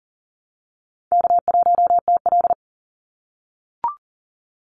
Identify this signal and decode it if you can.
Morse code, heard as a keyed tone, about 40 wpm, 700 Hz. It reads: K1TL
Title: Call Signs - 40wpm